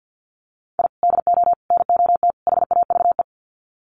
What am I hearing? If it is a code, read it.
Morse code, heard as a keyed tone, about 50 wpm, 700 Hz. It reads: I DO NOT HAVE
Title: Sets of 4 Words 50wpm